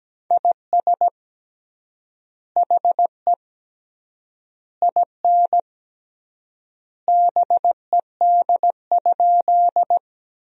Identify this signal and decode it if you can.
Morse code, heard as a keyed tone, about 17 wpm, 700 Hz. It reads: IS HE IN BED?
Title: 3x Word-Spacing - Sentences from Top 500 Words - 17wpm